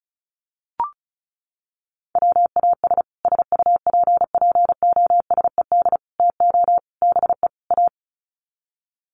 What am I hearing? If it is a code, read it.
Morse code, heard as a keyed tone, about 34 wpm, 700 Hz. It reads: WAS SUPPOSED TO BE A
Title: Sets of 5 Words Encore 35wpm